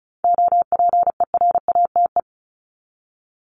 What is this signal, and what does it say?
Morse code, heard as a keyed tone, about 34 wpm, 700 Hz. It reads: OPERATE
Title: ING Suffix - Top 100 Words with Base Words 35wpm